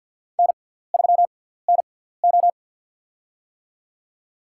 Morse code, heard as a keyed tone, about 52 wpm, spaced as for 17 wpm, 700 Hz. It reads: N3NO
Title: USA Call Signs - 4 Characters - 6x Character Spacing - 50wpm